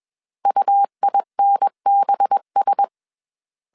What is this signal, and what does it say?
Morse code, heard as a keyed tone, about 21 wpm, 800 Hz. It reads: UINBS
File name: CW / DET / 1 6 1500C S A 20wpm 2048